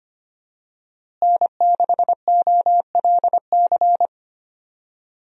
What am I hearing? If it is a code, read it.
Morse code, heard as a keyed tone, about 25 wpm, 700 Hz. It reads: N6OLC
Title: USA Call Signs - 5 Characters - Repeated 2x - 25wpm